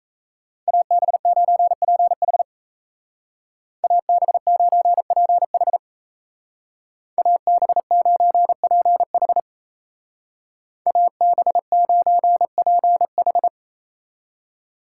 Morse code, heard as a keyed tone, about 36 wpm, 700 Hz. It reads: AB9PH AB9PH AB9PH AB9PH